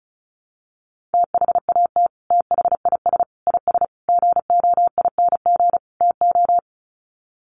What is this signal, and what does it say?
Morse code, heard as a keyed tone, about 34 wpm, 700 Hz. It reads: THAT THIS IS GOING TO
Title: Sets of 5 Words Encore2 35wpm